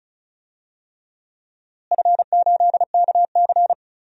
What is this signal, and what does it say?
Morse code, heard as a keyed tone, about 34 wpm, 700 Hz. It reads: F8KC
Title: DXCC-100 Call Signs - 4 Characters - Repeated 3x - 35wpm